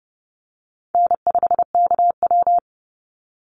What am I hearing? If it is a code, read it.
Morse code, heard as a keyed tone, about 30 wpm, 700 Hz. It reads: N5KW
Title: Easy Call Signs - 30wpm